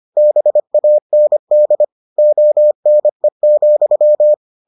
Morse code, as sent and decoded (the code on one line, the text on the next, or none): -... .- -. -.. / --- -. --..--
BAND ONE,